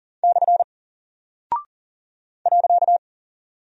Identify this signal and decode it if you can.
Morse code, heard as a keyed tone, about 40 wpm, 700 Hz. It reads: / .
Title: Single Character (Letter:Number:Punctuation:Prosign) Alt 40wpm